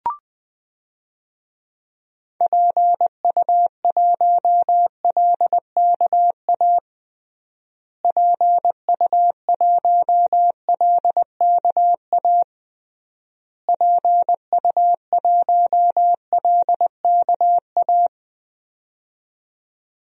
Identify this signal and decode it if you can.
Morse code, heard as a keyed tone, about 20 wpm, 700 Hz. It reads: PU1LKA PU1LKA PU1LKA
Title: DXCC-100 Call Signs - 6 Characters - Repeated 3x - 20wpm